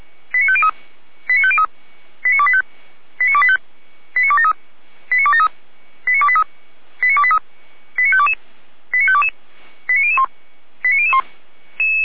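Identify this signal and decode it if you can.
Five-tone selective call (ZVEI), heard as quick runs of five tones, tones of about 70 ms. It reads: 89503 89503 89285 89285 89273 89273 89283 89283 89522 89522 89001 89001
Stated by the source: ZVEI five-tone selective call 89170, received by SDR.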